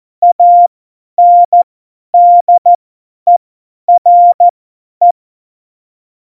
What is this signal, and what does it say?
Morse code, heard as a keyed tone, about 14 wpm, spaced as for 10 wpm, 700 Hz. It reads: ANDERE
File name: Morsekurs / Witze / Witz 25. Dezember, 50 F70